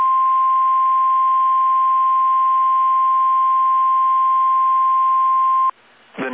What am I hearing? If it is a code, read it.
Emergency Alert System (SAME) tones (the 1050 Hz attention tone). It is heard from 0.8 to 5.3 s.